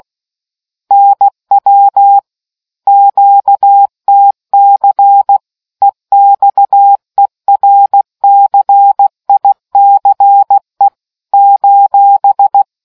Morse code, heard as a keyed tone, about 16 wpm, 800 Hz. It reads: NW QTC EXERCICE :